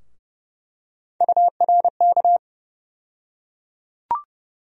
Morse code, heard as a keyed tone, about 31 wpm, 700 Hz. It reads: URK
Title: USA Call Sign Postfixes - 3 Letters 30wpm